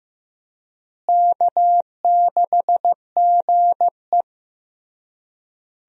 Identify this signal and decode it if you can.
Morse code, heard as a keyed tone, about 15 wpm, 700 Hz. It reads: K6GE
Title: USA Call Signs - 4 Characters - Repeated 3x - 15wpm